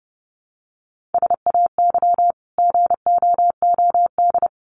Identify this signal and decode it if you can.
Morse code, heard as a keyed tone, about 30 wpm, 700 Hz. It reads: SAY GOOD
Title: Sentences from Top 300 Words 30wpm